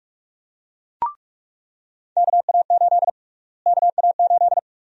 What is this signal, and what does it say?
Morse code, heard as a keyed tone, about 46 wpm, 700 Hz. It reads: KA8 KA8
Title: USA Call Sign Prefixes - Repeated 2x - 45wpm